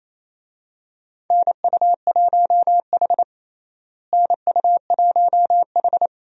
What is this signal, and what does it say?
Morse code, heard as a keyed tone, about 27 wpm, 700 Hz. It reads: NU1H NU1H
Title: USA Call Signs - 4 Characters - Repeated 3x - 28wpm